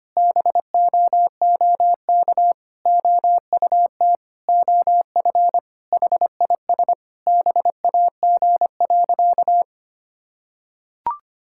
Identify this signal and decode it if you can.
Morse code, heard as a keyed tone, about 25 wpm, 700 Hz. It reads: BOOK OUT OF HIS BAG.
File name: practice / 25wpm / Sentences with 3-Word Phrases - 25wpm